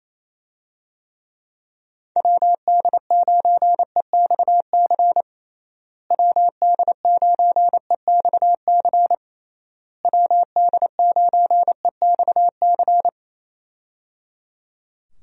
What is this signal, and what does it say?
Morse code, heard as a keyed tone, about 27 wpm, 700 Hz. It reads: WD9EXC WD9EXC WD9EXC
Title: USA Call Signs - 6 Characters - Repeated 3x - 28wpm